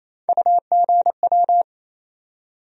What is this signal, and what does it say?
Morse code, heard as a keyed tone, about 28 wpm, 700 Hz. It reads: UGW
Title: USA Call Sign Postfixes - 3 Letters 28wpm